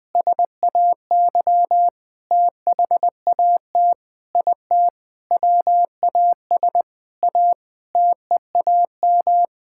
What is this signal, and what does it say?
Morse code, heard as a keyed tone, about 21 wpm, 700 Hz. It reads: SAY THAT IT WAS A TEAM
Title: Sentences with 8-Word Phrases - 20wpm